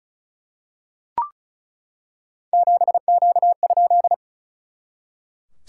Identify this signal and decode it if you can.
Morse code, heard as a keyed tone, about 36 wpm, 690 Hz. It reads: 7Q?
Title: SOTA QSO Practice 35wpm